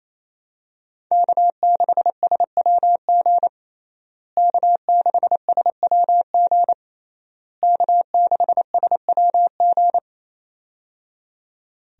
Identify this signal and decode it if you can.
Morse code, heard as a keyed tone, about 28 wpm, 700 Hz. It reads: K6SWG K6SWG K6SWG
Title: USA Call Signs - 5 Characters - Repeated 3x - 28wpm